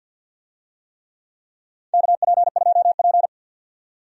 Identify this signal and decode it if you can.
Morse code, heard as a keyed tone, about 53 wpm, 700 Hz. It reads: KP2P
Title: DXCC-100 Call Signs - 4 Characters - 50wpm